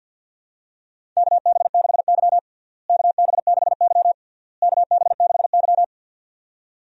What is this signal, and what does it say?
Morse code, heard as a keyed tone, about 52 wpm, 700 Hz. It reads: KB6Y KB6Y KB6Y
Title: USA Call Signs - 4 Characters - Repeated 3x - 50wpm